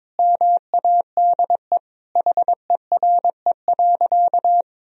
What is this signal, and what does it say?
Morse code, heard as a keyed tone, about 22 wpm, 700 Hz. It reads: MADE HERE.
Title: Sentences from Top 1000 Words - Review Part 4 - 22wpm